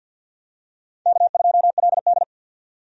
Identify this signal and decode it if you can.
Morse code, heard as a keyed tone, about 51 wpm, 700 Hz. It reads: K2LD
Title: USA Call Signs - 4 Characters - 50wpm